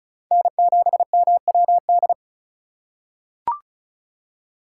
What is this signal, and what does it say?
Morse code, heard as a keyed tone, about 36 wpm, 700 Hz. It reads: N7MWD
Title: USA Call Signs - 8x Character Spacing - 35wpm